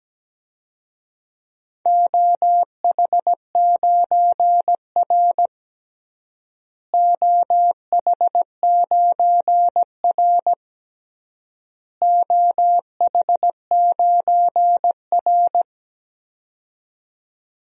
Morse code, heard as a keyed tone, about 17 wpm, 700 Hz. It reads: OH9R OH9R OH9R
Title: DXCC-100 Call Signs - 4 Characters - Repeated 3x - 17wpm